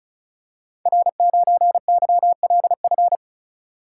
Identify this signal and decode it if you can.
Morse code, heard as a keyed tone, about 36 wpm, 700 Hz. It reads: R9YLF
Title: Code Groups - 5 Letter:Numbers - 35wpm